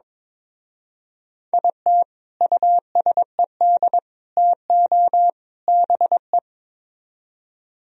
Morse code, heard as a keyed tone, about 22 wpm, 700 Hz. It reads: IT USED TO BE
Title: Speed-Racing - Sets of 4 Words 22wpm